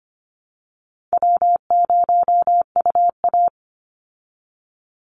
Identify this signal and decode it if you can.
Morse code, heard as a keyed tone, about 25 wpm, 700 Hz. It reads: W0UA
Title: Easy Call Signs Repeated 3x - 25wpm